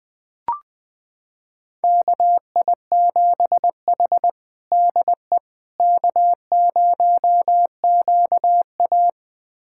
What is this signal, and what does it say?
Morse code, heard as a keyed tone, about 20 wpm, 700 Hz. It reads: KI7H DE K0QA
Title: USA Callsign DE Callsign - 4 Characters - 20wpm